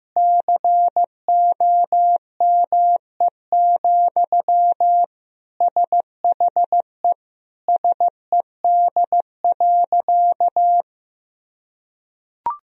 Morse code, heard as a keyed tone, about 16 wpm, 700 Hz. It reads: COME, SHE SED.
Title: Sentences from Top 100 Words and Abbreviations 15wpm